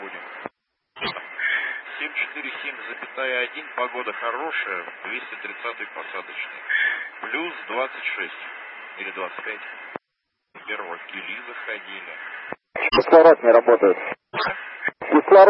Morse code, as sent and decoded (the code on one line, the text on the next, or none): none